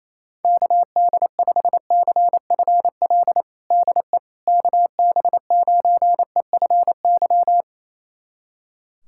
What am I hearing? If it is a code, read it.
Morse code, heard as a keyed tone, about 28 wpm, 700 Hz. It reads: KD5CFL DE KB9EFY
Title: USA Callsign DE Callsign 28wpm